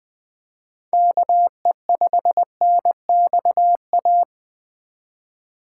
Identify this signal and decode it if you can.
Morse code, heard as a keyed tone, about 20 wpm, 700 Hz. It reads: KE5NXA
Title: USA Call Signs - 4x Character Spacing - 20wpm